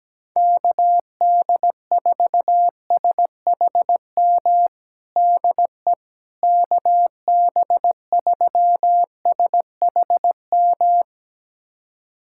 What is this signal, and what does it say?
Morse code, heard as a keyed tone, about 17 wpm, 700 Hz. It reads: KD4SHM DE KB3SHM
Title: USA Callsign DE Callsign - 6 Characters - 17wpm